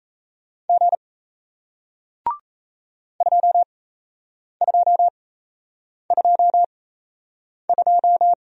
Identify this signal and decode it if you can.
Morse code, heard as a keyed tone, about 37 wpm, spaced as for 8 wpm, 700 Hz. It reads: G 2222